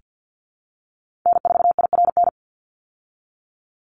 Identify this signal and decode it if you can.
Morse code, heard as a keyed tone, about 48 wpm, 700 Hz. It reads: N4IRN
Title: Call Signs Repeated 3x - 50wpm